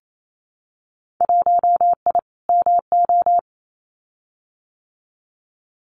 Morse code, heard as a keyed tone, about 28 wpm, 700 Hz. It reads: 1I MO
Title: Winter Field Day Exchanges Encore 28wpm